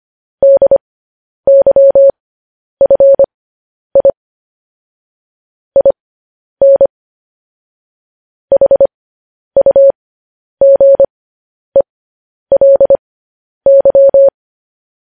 Morse code, heard as a keyed tone, about 25 wpm, spaced as for 10 wpm, 550 Hz. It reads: DYFI IN HUGELY
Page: News Headlines in Morse Code at 10 WPM